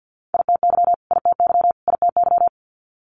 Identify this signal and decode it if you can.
Morse code, heard as a keyed tone, about 50 wpm, 700 Hz. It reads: ITY ITY ITY
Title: Multi-letter Phonemes Repeated 3x 50wpm